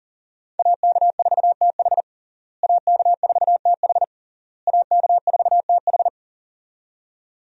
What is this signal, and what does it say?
Morse code, heard as a keyed tone, about 40 wpm, 700 Hz. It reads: AK4TH AK4TH AK4TH
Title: USA Call Signs - 5 Characters - Repeated 3x - 40wpm